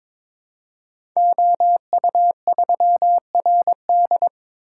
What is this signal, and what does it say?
Morse code, heard as a keyed tone, about 22 wpm, 700 Hz. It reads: OU3RD
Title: Code Groups - 5 Letter:Numbers - 22wpm